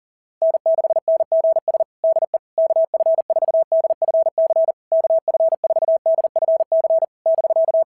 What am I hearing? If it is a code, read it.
Morse code, heard as a keyed tone, about 40 wpm, 650 Hz. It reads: N6NGS DE KF4DFC KF4DFC <BK>
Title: SOTA QSO Practice 40wpm